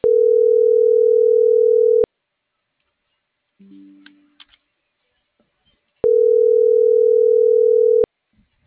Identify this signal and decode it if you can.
Ringback tone (North America), 440 + 480 Hz.